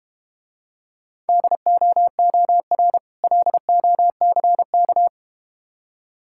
Morse code, heard as a keyed tone, about 31 wpm, 700 Hz. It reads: DOOR LOCK